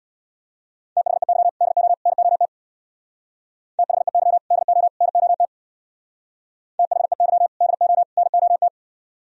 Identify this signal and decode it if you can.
Morse code, heard as a keyed tone, about 61 wpm, 700 Hz. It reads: THEY DO NOT THEY DO NOT THEY DO NOT
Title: Speed-Racing - Sets of 3 Words 50wpm